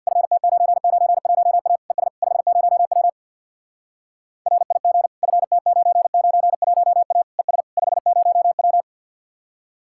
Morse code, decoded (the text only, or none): FT991A ES 50W RIG FT991A ES 50W